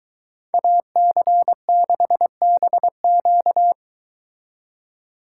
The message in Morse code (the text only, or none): AC6BQ